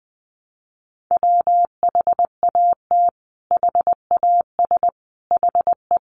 Morse code, heard as a keyed tone, about 19 wpm, 700 Hz. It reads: WHAT HAS HE
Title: Sentences from Top 500 Words 20wpm